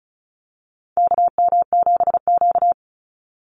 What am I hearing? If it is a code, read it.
Morse code, heard as a keyed tone, about 34 wpm, 700 Hz. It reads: KM7Q
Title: Call Signs - 35wpm